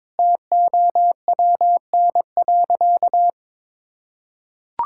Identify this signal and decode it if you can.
Morse code, heard as a keyed tone, about 22 wpm, 700 Hz. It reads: TOWN.
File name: practice / 22wpm / Sentences with 3-Word Phrases - 22wpm